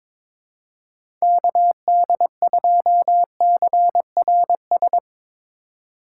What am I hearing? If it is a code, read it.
Morse code, heard as a keyed tone, about 22 wpm, 700 Hz. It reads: KD2CRS
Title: Speed Racing - USA 2x3 Call Signs 22wpm